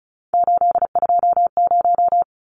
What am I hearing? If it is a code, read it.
Morse code, heard as a keyed tone, about 34 wpm, 700 Hz. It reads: 820
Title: Numbers - 3-Digit 35wpm